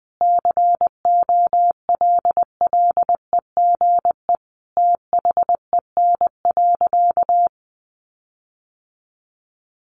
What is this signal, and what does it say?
Morse code, heard as a keyed tone, about 20 wpm, 700 Hz. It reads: COLLEGE THEN.
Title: Sentences from Top 1300 Words 20wpm